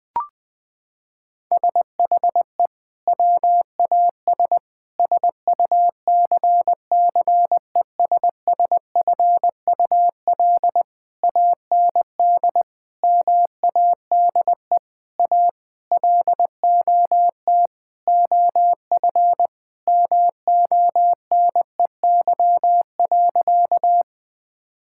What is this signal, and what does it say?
Morse code, heard as a keyed tone, about 21 wpm, 700 Hz. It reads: SHE WAS SUCCESSFUL AND MADE A LOT OF MONEY.
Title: Sentences from Top 1000 Words - Review Part 6 - 20wpm